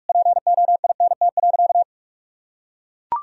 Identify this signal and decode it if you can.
Morse code, heard as a keyed tone, about 41 wpm, 700 Hz. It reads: POINT.
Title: Sentences from Top 1000 Words - Review Part 2 - 45wpm